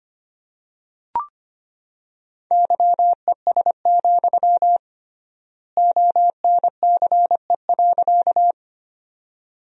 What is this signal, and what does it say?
Morse code, heard as a keyed tone, about 25 wpm, 700 Hz. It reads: YES, ONCE.
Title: 3x Word-Spacing - Sentences from Top 500 Words - 25wpm